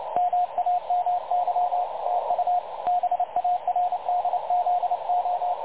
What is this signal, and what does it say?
Morse code, heard as a keyed tone, about 29 wpm, 720 Hz, with radio noise. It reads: MAMY TU BARDZO
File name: dlazaawansowanych / nagraniazpasma / websdr recording 3565.0kHz